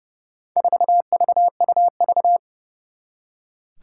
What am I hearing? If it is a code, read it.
Morse code, heard as a keyed tone, about 30 wpm, 690 Hz. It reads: 4VUV
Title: Four Challenging Letter-Number Combinations 30wpm